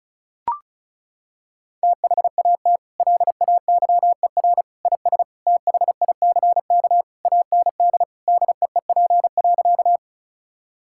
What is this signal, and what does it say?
Morse code, heard as a keyed tone, about 36 wpm, 700 Hz. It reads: THAT LAYER IS THICK AND DEEP.
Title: Sentences from Top 1600 Words 35wpm